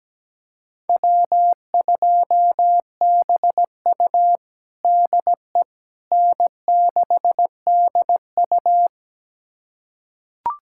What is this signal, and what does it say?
Morse code, heard as a keyed tone, about 17 wpm, 700 Hz. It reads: W2BU DE N6DU
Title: USA Callsign DE Callsign - 4 Characters - 17wpm